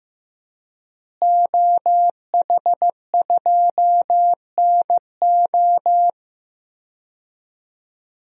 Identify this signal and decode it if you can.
Morse code, heard as a keyed tone, about 16 wpm, 700 Hz. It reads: OH2NO
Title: DXCC-100 Call Signs - 5 Characters - Repeated 2x - 15wpm